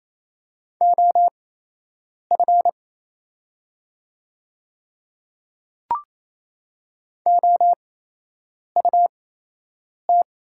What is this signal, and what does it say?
Morse code, heard as a keyed tone, about 28 wpm, spaced as for 8 wpm, 690 Hz. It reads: OF OUT
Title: 8x Character Spacing - Top 100 Words - 28wpm